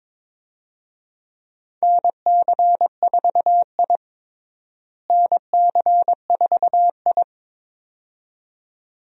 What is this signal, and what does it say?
Morse code, heard as a keyed tone, about 22 wpm, 700 Hz. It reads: NC4I NC4I
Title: USA Call Signs - 4 Characters - Repeated 2x - 22wpm